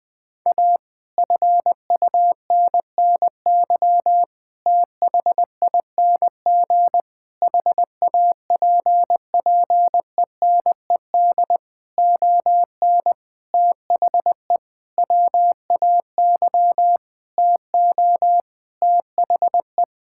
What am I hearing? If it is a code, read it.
Morse code, heard as a keyed tone, about 20 wpm, 700 Hz. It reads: A FUNNY THING HAPPENED ON THE WAY TO THE
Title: Sentences with 9-Word Phrases - 20wpm